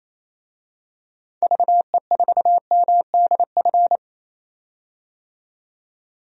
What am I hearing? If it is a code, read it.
Morse code, heard as a keyed tone, about 28 wpm, 700 Hz. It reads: VE4MDF
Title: DXCC-100 Call Signs - 6 Characters - 28wpm